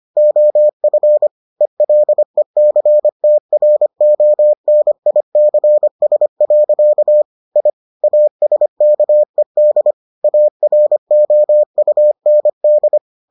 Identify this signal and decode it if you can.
Morse code, heard as a keyed tone, about 25 wpm, 600 Hz. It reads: OF ELECTRONICS. I ASKED AROUND